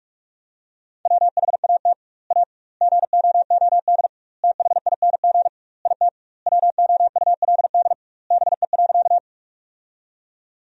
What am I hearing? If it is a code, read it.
Morse code, heard as a keyed tone, about 46 wpm, 700 Hz. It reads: WHAT A GOOD THING IT WOULD BE.
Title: 2x Word-Spacing - Sentences from Top 200 Words - 45wpm